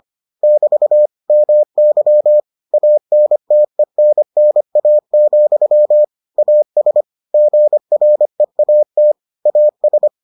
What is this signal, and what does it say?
Morse code, heard as a keyed tone, about 25 wpm, 600 Hz. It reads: <BT> MY ANTENNA, AS GREAT AS